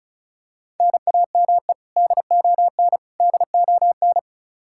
Morse code, heard as a keyed tone, about 36 wpm, 700 Hz. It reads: NAME DON DON
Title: Practice QSO - Exchange 1 - Beginning - 35wpm